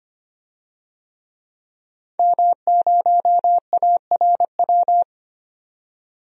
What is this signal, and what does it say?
Morse code, heard as a keyed tone, about 25 wpm, 700 Hz. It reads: M0ARW
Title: DXCC-100 Call Signs - 5 Characters - 25wpm